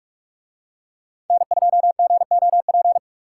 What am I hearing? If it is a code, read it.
Morse code, heard as a keyed tone, about 44 wpm, 700 Hz. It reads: N2GOP